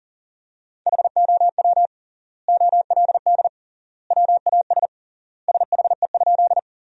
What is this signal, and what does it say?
Morse code, heard as a keyed tone, about 40 wpm, 700 Hz. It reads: HOW OLD WAS SHE?
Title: 3x Word-Spacing - Sentences from Top 200 Words - 40wpm